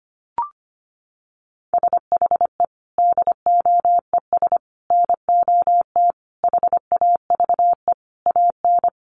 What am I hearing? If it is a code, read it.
Morse code, heard as a keyed tone, about 25 wpm, 700 Hz. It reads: SHE DOES NOT HAVE AN
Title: Sentences from Top 1200 Words 25wpm